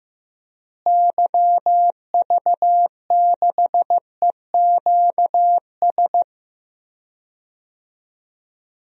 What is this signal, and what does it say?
Morse code, heard as a keyed tone, about 15 wpm, 700 Hz. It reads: YV6EQS
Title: DXCC-100 Call Signs - 6 Characters - Repeated 3x - 15wpm